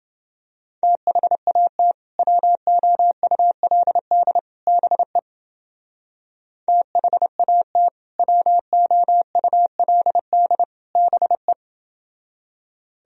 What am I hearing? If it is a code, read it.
Morse code, heard as a keyed tone, about 28 wpm, 700 Hz. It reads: THAT WOULD BE THAT WOULD BE